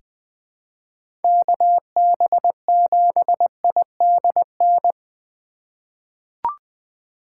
Morse code, heard as a keyed tone, about 20 wpm, 700 Hz. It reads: KB7IDN
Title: USA Call Signs - 6 Characters - 2x Character Spacing - 20wpm